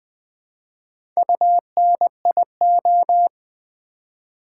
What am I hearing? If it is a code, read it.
Morse code, heard as a keyed tone, about 20 wpm, 700 Hz. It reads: UNIO
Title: Build A Word - 1 Letter At A Time - 5-Letter Words - 20wpm